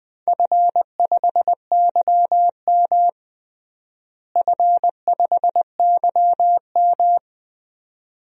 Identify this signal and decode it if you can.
Morse code, heard as a keyed tone, about 20 wpm, 700 Hz. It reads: F5YM F5YM
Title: DXCC-100 Call Signs - 4 Characters - Repeated 2x - 20wpm